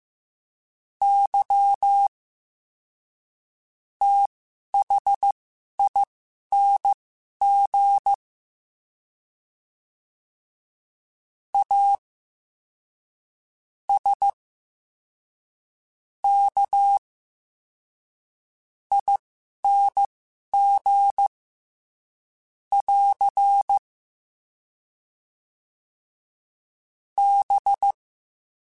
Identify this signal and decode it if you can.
Morse code, heard as a keyed tone, about 15 wpm, spaced as for 11 wpm, 800 Hz. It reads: Y THING A S K ING <AR> B
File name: Special INGs-mp3s / Special INGs-mp3s / SlowToFaster ing @10